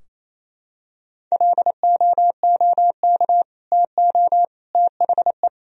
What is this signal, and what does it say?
Morse code, heard as a keyed tone, about 28 wpm, 700 Hz. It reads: LOOK TO THE
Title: Sentences from Top 1000 Words - Review Part 3 - 28wpm